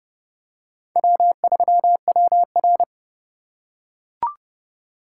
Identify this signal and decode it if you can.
Morse code, heard as a keyed tone, about 28 wpm, 700 Hz. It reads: W3WR